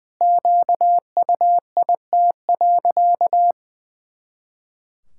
Morse code, heard as a keyed tone, about 20 wpm, 700 Hz. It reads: QUIT.
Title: Pangram Challenge v2 - 20wpm